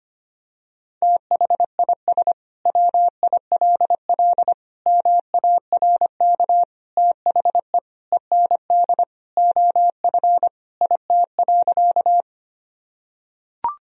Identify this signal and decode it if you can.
Morse code, heard as a keyed tone, about 25 wpm, 700 Hz. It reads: THIS WILL MARK THE END OF IT.
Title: Sentences from Top 1000 Words - Review Part 1 - 25wpm